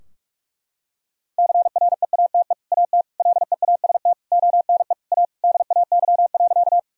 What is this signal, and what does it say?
Morse code, heard as a keyed tone, about 46 wpm, 700 Hz. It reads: CREATE AT LEAST ONE A DAY.
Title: Sentences from Top 1000 Words - Review Part 3 - 45wpm